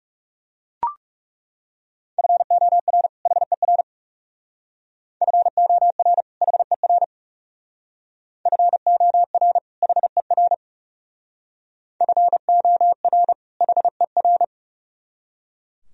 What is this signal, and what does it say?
Morse code, heard as a keyed tone, about 36 wpm, 700 Hz. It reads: FOR HER FOR HER FOR HER FOR HER
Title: Speed-Racing - Sets of 2 Words 30wpm